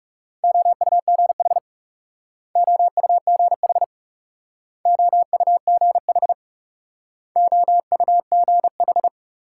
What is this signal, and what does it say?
Morse code, heard as a keyed tone, about 41 wpm, 700 Hz. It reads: OUGH OUGH OUGH OUGH